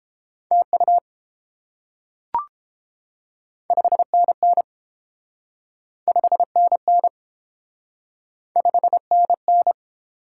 Morse code, heard as a keyed tone, about 30 wpm, 700 Hz. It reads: TU 5NN 5NN 5NN